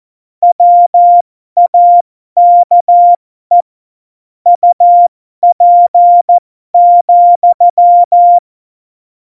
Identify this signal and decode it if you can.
Morse code, heard as a keyed tone, about 14 wpm, spaced as for 12 wpm, 700 Hz. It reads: WAKE UP,
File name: Morsekurs / Witze